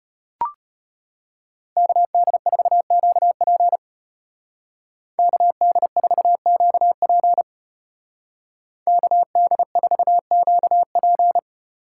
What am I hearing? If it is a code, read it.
Morse code, heard as a keyed tone, about 34 wpm, 700 Hz. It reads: KD4QP KD4QP KD4QP